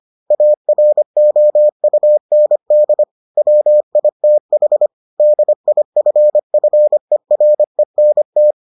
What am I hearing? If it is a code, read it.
Morse code, heard as a keyed tone, about 25 wpm, 600 Hz. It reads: AROUND WITH DIFFERENT